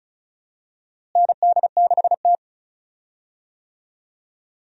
Morse code, heard as a keyed tone, about 36 wpm, 700 Hz. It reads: ND6T
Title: USA Call Signs - 4 Characters - Repeated 2x - 35wpm